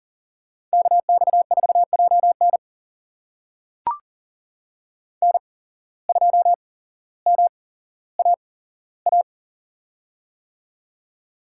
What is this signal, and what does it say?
Morse code, heard as a keyed tone, about 39 wpm, spaced as for 17 wpm, 700 Hz. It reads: KX4JN N2MAA
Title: USA Call Signs - 5 Characters - 8x Character Spacing - 40wpm